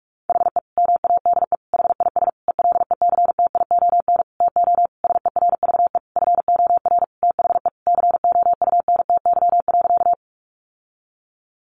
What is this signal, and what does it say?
Morse code, heard as a keyed tone, about 44 wpm, 700 Hz. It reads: HE MADE HIS ELECTION TO SERVE FOR THE COUNTY.
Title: Sentences from Top 1500 Words 45wpm